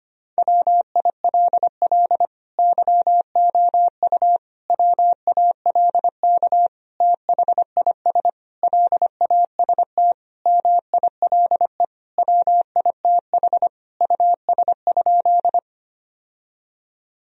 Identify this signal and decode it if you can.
Morse code, heard as a keyed tone, about 25 wpm, 700 Hz. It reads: WILL YOU WALK THIS LAST MILE WITH US?
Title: Sentences from Top 1000 Words - Review Part 1 - 25wpm